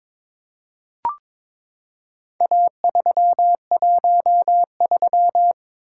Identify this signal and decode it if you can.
Morse code, heard as a keyed tone, about 22 wpm, 700 Hz. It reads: A313
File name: practice / 22wpm / Four Letter-Number Combinations - 22wpm